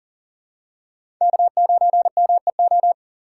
Morse code, heard as a keyed tone, about 39 wpm, 700 Hz. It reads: K9MEO
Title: USA Call Signs - 5 Characters - 3x Character Spacing - 40wpm